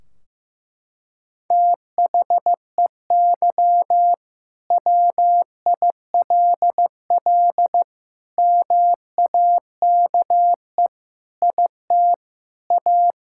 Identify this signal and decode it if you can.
Morse code, heard as a keyed tone, about 15 wpm, 700 Hz. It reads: THEY WILL MAKE IT A
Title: Sentences from Top 1600 Words 15wpm